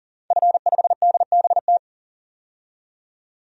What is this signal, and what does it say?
Morse code, heard as a keyed tone, about 40 wpm, 700 Hz. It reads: F5DBT
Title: DXCC-100 Call Signs - 5 Characters - 40wpm